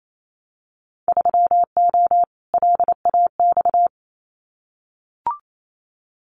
Morse code, heard as a keyed tone, about 28 wpm, 700 Hz. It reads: 3O LAX
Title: Winter Field Day Exchanges Encore 28wpm